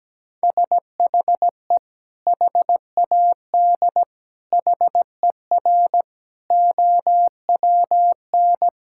Morse code, heard as a keyed tone, about 17 wpm, 700 Hz. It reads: SHE HAD HER OWN
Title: Sentences from Top 1000 Words - Review Part 5 - 17wpm